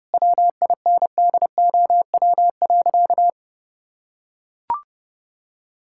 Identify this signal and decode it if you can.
Morse code, heard as a keyed tone, about 32 wpm, 700 Hz. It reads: WINDOW.
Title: Sentences from Top 1600 Words 30wpm